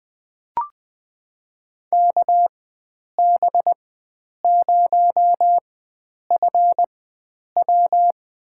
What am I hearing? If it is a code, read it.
Morse code, heard as a keyed tone, about 20 wpm, spaced as for 9 wpm, 700 Hz. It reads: KB0FW
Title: USA Call Signs - 4x Character Spacing - 20wpm